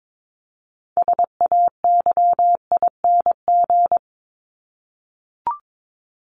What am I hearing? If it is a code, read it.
Morse code, heard as a keyed tone, about 22 wpm, 700 Hz. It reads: SAYING
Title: ING Suffix - Top 100 Words with Base Words 22wpm